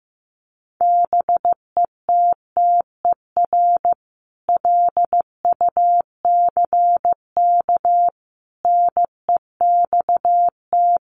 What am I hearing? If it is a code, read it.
Morse code, heard as a keyed tone, about 14 wpm, 700 Hz. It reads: BETTER LUCK NEXT
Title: Sentences from Top 1500 Words 15wpm